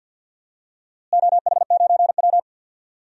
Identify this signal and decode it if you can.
Morse code, heard as a keyed tone, about 46 wpm, 700 Hz. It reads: OH9W
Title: DXCC-100 Call Signs - 4 Characters - Repeated 3x - 50wpm